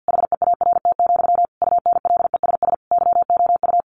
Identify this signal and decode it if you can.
Morse code, heard as a keyed tone, about 50 wpm, 700 Hz. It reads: HEART, UNLESS YOU
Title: Sentences from Top 1500 Words 50wpm